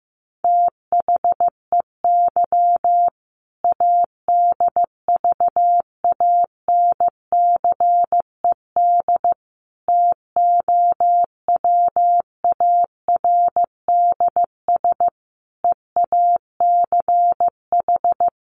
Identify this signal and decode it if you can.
Morse code, heard as a keyed tone, about 15 wpm, 700 Hz. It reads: THEY ADVANCED TOWARDS EACH
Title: Sentences from Top 1200 Words 15wpm